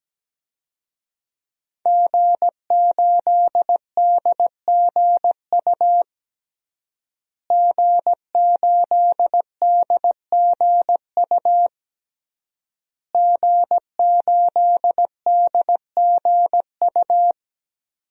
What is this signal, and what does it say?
Morse code, heard as a keyed tone, about 17 wpm, 700 Hz. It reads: G8DGU G8DGU G8DGU
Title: DXCC-100 Call Signs - 5 Characters - Repeated 3x - 17wpm